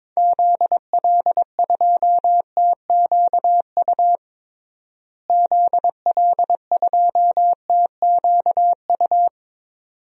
Morse code, heard as a keyed tone, about 22 wpm, 700 Hz. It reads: ZL2TQU ZL2TQU
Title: DXCC-100 Call Signs - 6 Characters - Repeated 2x - 22wpm